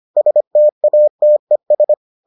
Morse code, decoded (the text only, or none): STATES